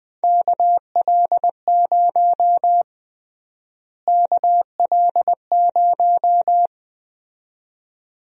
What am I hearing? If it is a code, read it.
Morse code, heard as a keyed tone, about 20 wpm, 700 Hz. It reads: KL0 KL0
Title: USA Call Sign Prefixes - Repeated 2x - 20wpm